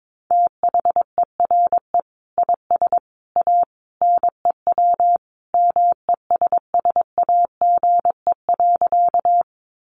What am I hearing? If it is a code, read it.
Morse code, heard as a keyed tone, about 22 wpm, 700 Hz. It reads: THERE IS A NEW MESSAGE.
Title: Sentences from Top 1100 Words 22wpm